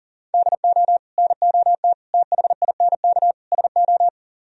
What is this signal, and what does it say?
Morse code, heard as a keyed tone, about 51 wpm, 700 Hz. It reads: DO NOT THINK SO